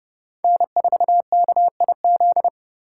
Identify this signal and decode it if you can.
Morse code, heard as a keyed tone, about 37 wpm, 700 Hz. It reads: N4KIZ
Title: Speed Racing - USA 1x3 Call Signs 30wpm